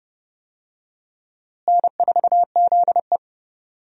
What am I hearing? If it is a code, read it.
Morse code, heard as a keyed tone, about 32 wpm, 700 Hz. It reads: N4ZE
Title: USA Call Signs - 4 Characters - Repeated 3x - 30wpm